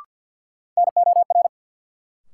Morse code, heard as a keyed tone, about 53 wpm, 700 Hz. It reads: NOR
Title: Sentences from Top 1500 Words - New Word Drill - 50wpm